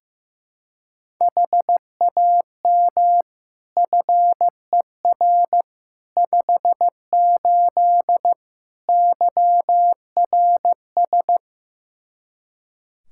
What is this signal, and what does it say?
Morse code, heard as a keyed tone, about 16 wpm, 700 Hz. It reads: HAM FER 58 YRS
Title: Practice QSO - Exchange 3 - More Info - 15wpm